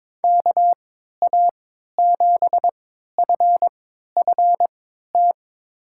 Morse code, heard as a keyed tone, about 21 wpm, spaced as for 12 wpm, 700 Hz. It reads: KA7FFT
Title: USA Call Signs - 6 Characters - 3x Character Spacing - 22wpm